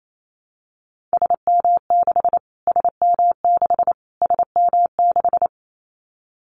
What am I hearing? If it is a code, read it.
Morse code, heard as a keyed tone, about 28 wpm, 700 Hz. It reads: SM6 SM6 SM6
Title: Callsign Prefixes for Top 25 DXCC - Repeated 3x - 28wpm